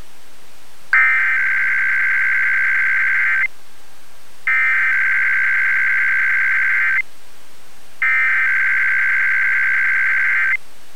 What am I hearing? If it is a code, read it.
Emergency Alert System (SAME) tones (header bursts), heard as harsh buzzing bursts. It is heard from 0.9 to 10.5 s.